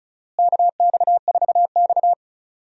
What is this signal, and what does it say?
Morse code, heard as a keyed tone, about 36 wpm, 700 Hz. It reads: KX4X
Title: USA Call Signs - 4 Characters - Repeated 3x - 35wpm